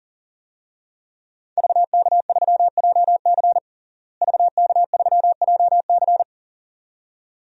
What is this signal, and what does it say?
Morse code, heard as a keyed tone, about 40 wpm, 700 Hz. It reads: VK3JC VK3JC
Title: DXCC-100 Call Signs - 5 Characters - Repeated 2x - 40wpm